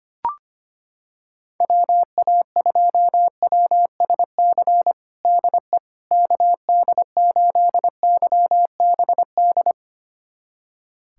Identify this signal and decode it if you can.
Morse code, heard as a keyed tone, about 25 wpm, 700 Hz. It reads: WA2WSC DE KD8YBD
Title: USA Callsign DE Callsign - 6 Characters - 25wpm